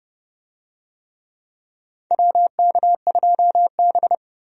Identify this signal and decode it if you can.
Morse code, heard as a keyed tone, about 31 wpm, 700 Hz. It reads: WK2B
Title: USA Call Signs - 4 Characters - 30wpm